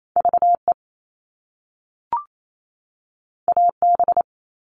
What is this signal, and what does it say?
Morse code, heard as a keyed tone, about 28 wpm, 700 Hz. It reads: VE AB
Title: All Cut Numbers - 2-Digit 28wpm